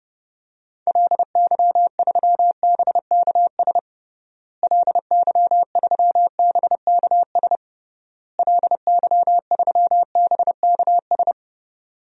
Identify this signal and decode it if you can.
Morse code, heard as a keyed tone, about 30 wpm, 700 Hz. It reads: LY3BKS LY3BKS LY3BKS
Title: DXCC-100 Call Signs - 6 Characters - Repeated 3x - 30wpm